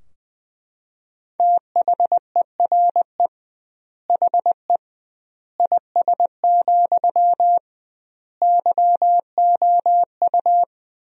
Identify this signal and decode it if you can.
Morse code, heard as a keyed tone, about 21 wpm, 700 Hz. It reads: THERE HE IS, YOU
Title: 2x Word-Spacing - Sentences from Top 100 Words - 20wpm